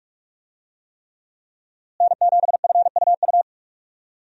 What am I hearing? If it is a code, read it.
Morse code, heard as a keyed tone, about 46 wpm, 700 Hz. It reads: N7FUU
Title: USA Call Signs - 5 Characters - Repeated 3x - 45wpm